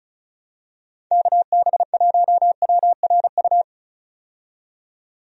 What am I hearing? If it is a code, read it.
Morse code, heard as a keyed tone, about 34 wpm, 700 Hz. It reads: KB1WRU
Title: USA Call Signs - 6 Characters - 35wpm